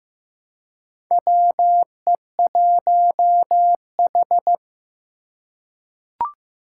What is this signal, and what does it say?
Morse code, heard as a keyed tone, about 15 wpm, 700 Hz. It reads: WE1H
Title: Four Challenging Letter-Number Combinations 15wpm